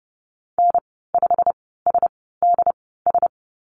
Morse code, heard as a keyed tone, about 30 wpm, spaced as for 17 wpm, 700 Hz. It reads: N5SDS
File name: Lesson 15g - Number 5 - Call Signs 30wpm